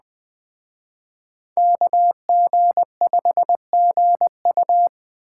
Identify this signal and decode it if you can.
Morse code, heard as a keyed tone, about 21 wpm, 700 Hz. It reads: KG5GU